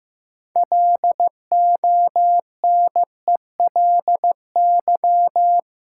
Morse code, heard as a keyed tone, about 15 wpm, 700 Hz. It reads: LONELY
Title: Six Letter Words - 15wpm